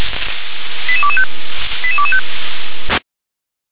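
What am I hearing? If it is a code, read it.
Five-tone selective call (ZVEI), heard as quick runs of five tones, tones of about 70 ms. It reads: 88225 88225
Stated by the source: ZVEI five-tone selective call received on an SDR.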